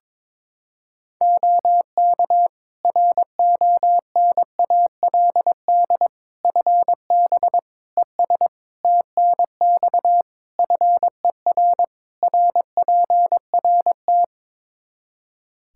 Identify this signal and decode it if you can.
Morse code, heard as a keyed tone, about 22 wpm, 700 Hz. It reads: OK RONALD FB ES TNX FER RPRT
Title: Practice QSO - Exchange 2 - Info - 22wpm